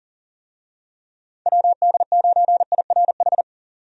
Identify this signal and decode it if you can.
Morse code, heard as a keyed tone, about 40 wpm, 700 Hz. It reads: WD9IRH
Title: USA Call Signs - 6 Characters - 40wpm